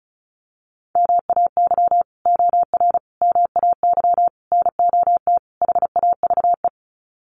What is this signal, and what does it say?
Morse code, heard as a keyed tone, about 34 wpm, 700 Hz. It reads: MAY OR MAY NOT HAVE
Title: Sets of 5 Words Encore2 35wpm